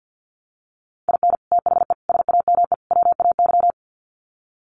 Morse code, heard as a keyed tone, about 50 wpm, 700 Hz. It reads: IN THE SAME WAY
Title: Sets of 4 Words 50wpm